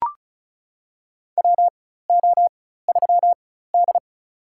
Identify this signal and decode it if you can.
Morse code, heard as a keyed tone, about 36 wpm, spaced as for 16 wpm, 700 Hz. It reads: WO3D
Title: USA Call Signs - 4 Characters - 4x Character Spacing - 35wpm